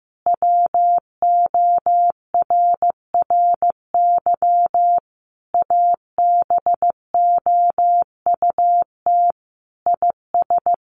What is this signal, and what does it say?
Morse code, heard as a keyed tone, about 15 wpm, 700 Hz. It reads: WORRY ABOUT IS
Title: A Sound Of Thunder 15wpm